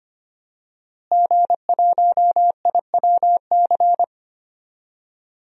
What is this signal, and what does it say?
Morse code, heard as a keyed tone, about 25 wpm, 700 Hz. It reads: G1IWC
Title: DXCC-100 Call Signs - 5 Characters - Repeated 3x - 25wpm